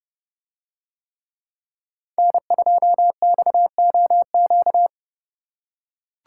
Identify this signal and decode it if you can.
Morse code, heard as a keyed tone, about 30 wpm, 700 Hz. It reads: N2XOQ